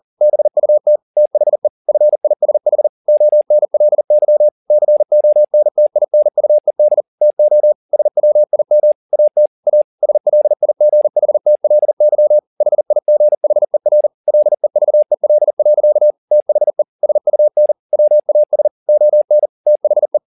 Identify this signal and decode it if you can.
Morse code, heard as a keyed tone, about 40 wpm, 600 Hz. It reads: BUT THE FISH ONLY CONTINUED TO SWIM AT A SLIGHTLY HIGHER LEVEL. THE SUN WAS ON THE